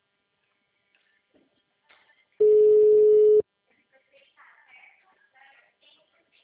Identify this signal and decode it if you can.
Ringback tone (Europe, 425 Hz).